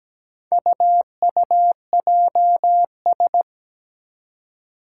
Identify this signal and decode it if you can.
Morse code, heard as a keyed tone, about 17 wpm, 700 Hz. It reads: UUJS
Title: Four Challenging Letter-Number Combinations 17wpm